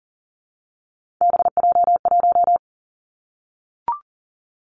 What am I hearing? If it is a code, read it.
Morse code, heard as a keyed tone, about 40 wpm, 690 Hz. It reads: BJ1
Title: Three Challenging Letter-Number Combinations 40wpm